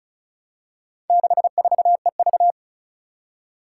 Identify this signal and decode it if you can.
Morse code, heard as a keyed tone, about 36 wpm, 700 Hz. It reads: 64EV
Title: Four Challenging Letter-Number Combinations 35wpm